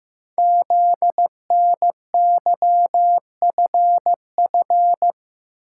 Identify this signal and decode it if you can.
Morse code, heard as a keyed tone, about 15 wpm, 700 Hz. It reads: ZNYFF
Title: Code Groups - 5 Letters - 15wpm